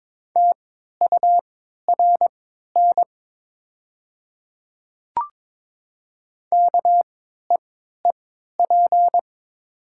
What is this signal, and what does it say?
Morse code, heard as a keyed tone, about 22 wpm, spaced as for 12 wpm, 700 Hz. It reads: TURN KEEP
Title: 3x Character Spacing - Top 300 Words - 22wpm